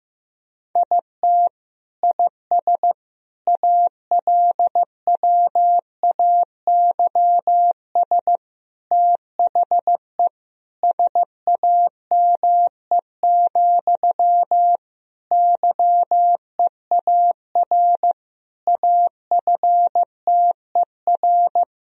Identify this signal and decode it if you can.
Morse code, heard as a keyed tone, about 15 wpm, 700 Hz. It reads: IT IS ALWAYS THE SAME, YEAR AFTER
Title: Sentences from Top 1000 Words - Review Part 1 - 15wpm